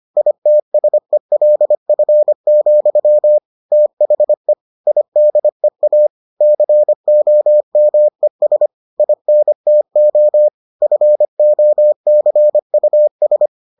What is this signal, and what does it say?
Morse code, heard as a keyed tone, about 25 wpm, 600 Hz. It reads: ITSELF, THE IDEA COMES INTO FOCUS